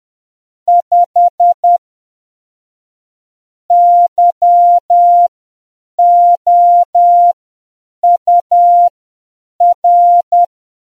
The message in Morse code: 5 YOUR